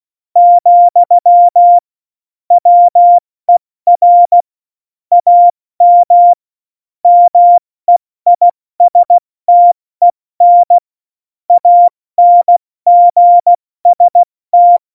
Morse code, heard as a keyed tone, about 16 wpm, spaced as for 14 wpm, 700 Hz. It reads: , WER AM MEISTEN ANGST
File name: Morsekurs / Witze / Witz 18. Dezember, 70 F80